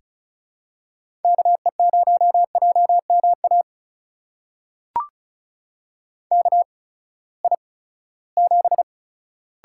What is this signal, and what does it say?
Morse code, heard as a keyed tone, about 35 wpm, spaced as for 20 wpm, 700 Hz. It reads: KE0JMA KI7
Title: USA Call Signs - 6 Characters - 8x Character Spacing - 35wpm